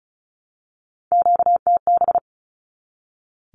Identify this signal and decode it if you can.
Morse code, heard as a keyed tone, about 34 wpm, 700 Hz. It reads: QTB
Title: Q-Codes 35wpm